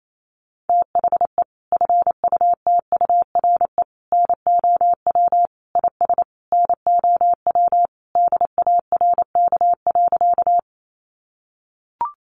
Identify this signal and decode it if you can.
Morse code, heard as a keyed tone, about 28 wpm, 700 Hz. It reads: THE FUTURE NOW IS NOW DARK.
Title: Sentences from Top 700 Words 28wpm